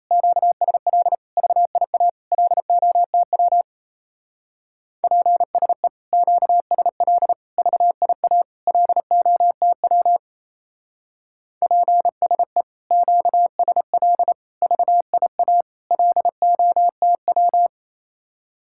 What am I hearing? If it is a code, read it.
Morse code, heard as a keyed tone, about 36 wpm, 700 Hz. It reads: QSL VIA LOTW PSE QSL VIA LOTW PSE QSL VIA LOTW